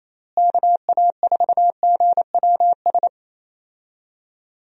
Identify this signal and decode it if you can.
Morse code, heard as a keyed tone, about 28 wpm, 700 Hz. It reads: KA4GWS
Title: USA Call Signs - 6 Characters - Repeated 2x - 28wpm